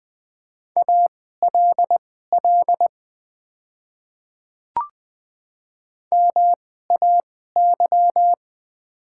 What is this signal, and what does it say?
Morse code, heard as a keyed tone, about 20 wpm, spaced as for 14 wpm, 700 Hz. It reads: ALL MAY
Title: 2x Character Spacing - Top 300 Words - 20wpm